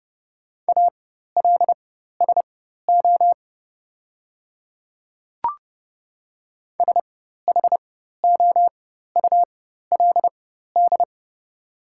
Morse code, heard as a keyed tone, about 30 wpm, spaced as for 14 wpm, 700 Hz. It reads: ALSO SHOULD
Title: 4x Character Spacing - Top 500 Words - 30wpm